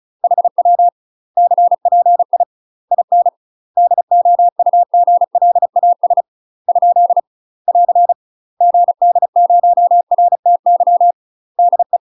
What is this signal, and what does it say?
Morse code, heard as a keyed tone, about 34 wpm, 700 Hz. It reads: HW CPI IN DOUGLAS ? <AR> GD0RTY DE
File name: UK MASTER QSOs TRAINING / 35WPM / UK MASTER QSOs 0040